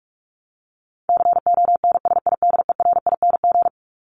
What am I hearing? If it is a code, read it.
Morse code, heard as a keyed tone, about 44 wpm, 700 Hz. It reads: CONSIDERING
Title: ING Suffix - Top 100 Words with Base Words 45wpm